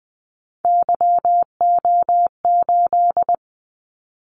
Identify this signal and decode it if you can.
Morse code, heard as a keyed tone, about 20 wpm, 700 Hz. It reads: YO8
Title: Callsign Prefixes for Top 25 DXCC - 20wpm